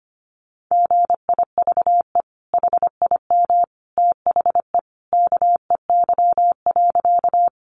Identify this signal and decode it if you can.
Morse code, heard as a keyed tone, about 25 wpm, 700 Hz. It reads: GIVE HIM THE KEY.
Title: Sentences from Top 600 Words 25wpm